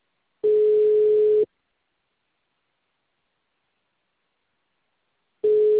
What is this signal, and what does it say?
Ringback tone (Europe, 425 Hz).
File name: out- -2007- - - .8860